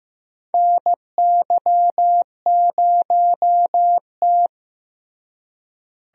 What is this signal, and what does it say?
Morse code, heard as a keyed tone, about 15 wpm, 700 Hz. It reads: NY0T
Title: USA Call Signs - 4 Characters - Repeated 3x - 15wpm